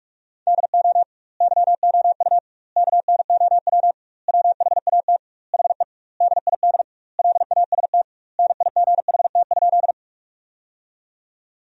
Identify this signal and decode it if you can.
Morse code, heard as a keyed tone, about 41 wpm, 700 Hz. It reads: DO YOU KNOW WHAT HE DID LAST NIGHT?
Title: 2x Word-Spacing - Sentences from Top 300 Words - 45wpm